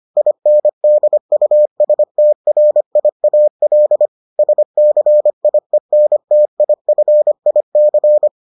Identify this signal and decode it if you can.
Morse code, heard as a keyed tone, about 25 wpm, 600 Hz. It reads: INDUSTRIAL SCIENTIFIC